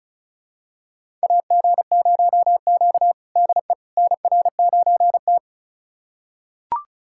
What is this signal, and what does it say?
Morse code, heard as a keyed tone, about 34 wpm, 700 Hz. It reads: AG0Q DE NR9T
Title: USA Callsign DE Callsign - 4 Characters - 35wpm